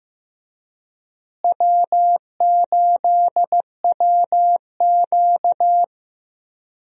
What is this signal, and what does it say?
Morse code, heard as a keyed tone, about 15 wpm, 700 Hz. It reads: W8WQ
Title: USA Call Signs - 4 Characters - Repeated 2x - 15wpm